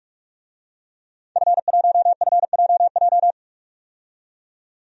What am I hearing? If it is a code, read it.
Morse code, heard as a keyed tone, about 46 wpm, 700 Hz. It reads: F1FJJ
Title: DXCC-100 Call Signs - 5 Characters - 45wpm